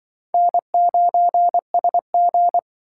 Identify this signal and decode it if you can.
Morse code, heard as a keyed tone, about 25 wpm, 700 Hz. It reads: N9SG
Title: Speed Racing - USA 1x2 Call Signs 20wpm